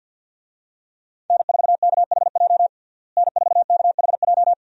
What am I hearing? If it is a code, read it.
Morse code, heard as a keyed tone, about 52 wpm, 700 Hz. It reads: N4KHJ N4KHJ
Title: USA Call Signs - 5 Characters - Repeated 2x - 50wpm